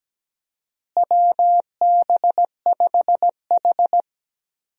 Morse code, heard as a keyed tone, about 17 wpm, 700 Hz. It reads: WB5H
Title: USA Call Signs - 4 Characters - Repeated 3x - 17wpm